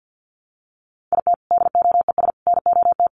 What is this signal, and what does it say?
Morse code, heard as a keyed tone, about 50 wpm, 700 Hz. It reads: IT DOES NOT
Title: Sets of 5 Words 50wpm